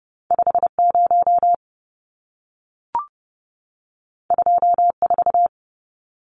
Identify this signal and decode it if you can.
Morse code, heard as a keyed tone, about 30 wpm, 700 Hz. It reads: 50 24
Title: Numbers - 2-Digit 30wpm